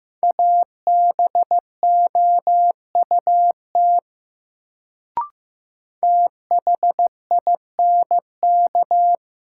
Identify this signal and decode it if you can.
Morse code, heard as a keyed tone, about 15 wpm, 700 Hz. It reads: ABOUT THINK
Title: Sets of 2 Words - New Word Drill - 15wpm